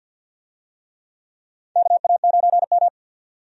Morse code, heard as a keyed tone, about 51 wpm, 700 Hz. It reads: KA9M
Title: USA Call Signs - 4 Characters - 50wpm